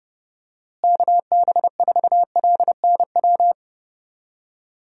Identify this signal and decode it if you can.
Morse code, heard as a keyed tone, about 30 wpm, 700 Hz. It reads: KB4LNW
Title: USA Call Signs - 6 Characters - Repeated 2x - 30wpm